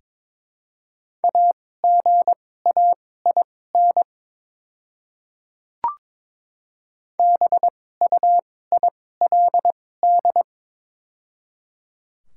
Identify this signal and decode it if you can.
Morse code, heard as a keyed tone, about 22 wpm, spaced as for 16 wpm, 700 Hz. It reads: AGAIN BUILD
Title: 2x Character Spacing - Top 200 Words - 22wpm